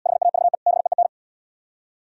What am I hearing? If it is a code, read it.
Morse code, heard as a keyed tone, about 77 wpm, 700 Hz. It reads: HAVE BEEN